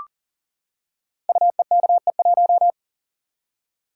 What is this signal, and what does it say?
Morse code, heard as a keyed tone, about 38 wpm, 690 Hz. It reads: UEKE1
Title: Code Groups - 5 Letter:Numbers - 40wpm